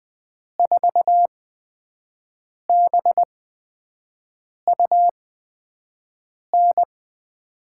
Morse code, heard as a keyed tone, about 20 wpm, spaced as for 5 wpm, 700 Hz. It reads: 4BUN
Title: USA Call Signs - 8x Character Spacing - 20wpm